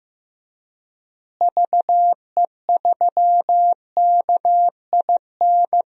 Morse code, heard as a keyed tone, about 15 wpm, 700 Hz. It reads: VE3KIN
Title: DXCC-100 Call Signs - 6 Characters - Repeated 3x - 15wpm